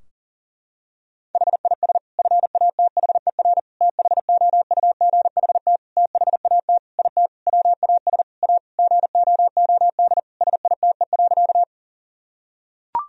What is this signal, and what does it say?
Morse code, heard as a keyed tone, about 40 wpm, 700 Hz. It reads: HIS FATHER THOUGHT THAT IT WAS A GOOD SITE.
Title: Sentences from Top 1000 Words - Review Part 4 - 40wpm